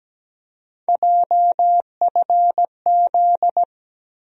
Morse code, heard as a keyed tone, about 17 wpm, 700 Hz. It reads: JFZ
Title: USA Call Sign Postfixes - 3 Letters 17wpm